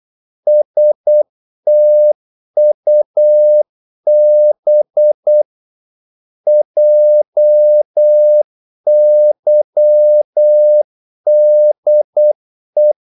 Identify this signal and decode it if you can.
Morse code, heard as a keyed tone, about 8 wpm, 600 Hz. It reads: STUB JYDE
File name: Korte ord DK 8wpm